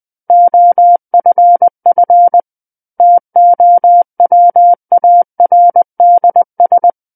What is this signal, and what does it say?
Morse code, heard as a keyed tone, about 20 wpm, 700 Hz. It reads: OFF TOWARDS
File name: 20 WPM